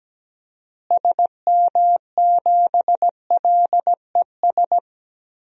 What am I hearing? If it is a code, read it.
Morse code, heard as a keyed tone, about 17 wpm, 700 Hz. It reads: SM7LES
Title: DXCC-100 Call Signs - 6 Characters - Repeated 2x - 17wpm